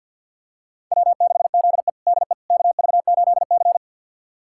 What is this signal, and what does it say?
Morse code, heard as a keyed tone, about 51 wpm, 700 Hz. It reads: W6ZE DE KV8C
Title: USA Callsign DE Callsign - 4 Characters - 50wpm